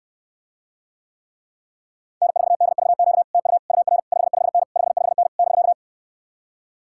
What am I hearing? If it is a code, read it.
Morse code, heard as a keyed tone, about 69 wpm, 700 Hz. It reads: N4DVQ TU UR 55N 55N <BK>